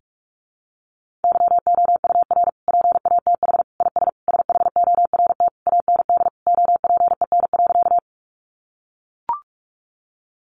Morse code, heard as a keyed tone, about 44 wpm, 700 Hz. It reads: YOUR PATH IS SHORT AND OPEN.
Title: Sentences from Top 1400 Words 45wpm